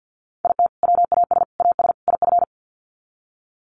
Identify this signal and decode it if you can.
Morse code, heard as a keyed tone, about 48 wpm, 700 Hz. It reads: IT WAS AS IF
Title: Sets of 4 Words 50wpm